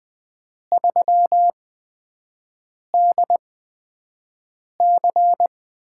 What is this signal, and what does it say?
Morse code, heard as a keyed tone, about 20 wpm, spaced as for 5 wpm, 700 Hz. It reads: 3DC
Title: USA Call Signs - 4 Characters - 8x Character Spacing - 20wpm